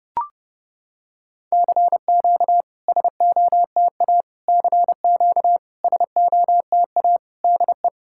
Morse code, heard as a keyed tone, about 30 wpm, 700 Hz. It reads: CQ SOTA CQ SOTA DE
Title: SOTA QSO Practice 30wpm